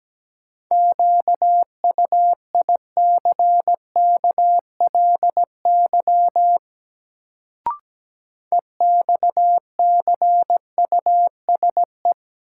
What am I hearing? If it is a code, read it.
Morse code, heard as a keyed tone, about 17 wpm, 700 Hz. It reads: QUICKLY EXCUSE
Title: Sentences from Top 700 Words - New Word Drill - 17wpm